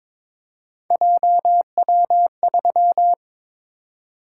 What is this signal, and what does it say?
Morse code, heard as a keyed tone, about 22 wpm, 700 Hz. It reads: JW3
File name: practice / 22wpm / Three Letter-Number Combinations - 22wpm